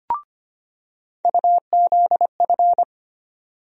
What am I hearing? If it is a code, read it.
Morse code, heard as a keyed tone, about 25 wpm, 700 Hz. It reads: UZF
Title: USA Call Sign Postfixes - 3 Letters 25wpm